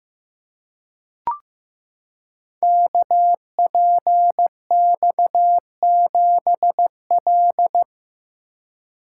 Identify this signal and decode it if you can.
Morse code, heard as a keyed tone, about 15 wpm, 700 Hz. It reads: KPX7L
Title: Code Groups - 5 Letter:Numbers - 15wpm